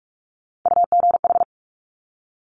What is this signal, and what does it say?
Morse code, heard as a keyed tone, about 46 wpm, 690 Hz. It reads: UGH
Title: Top Three Letter Combinations in Words 45wpm